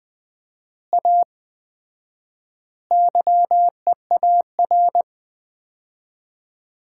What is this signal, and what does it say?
Morse code, heard as a keyed tone, about 20 wpm, 700 Hz. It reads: A YEAR
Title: 4x Word-Spacing - Sets of 4 Words - 20wpm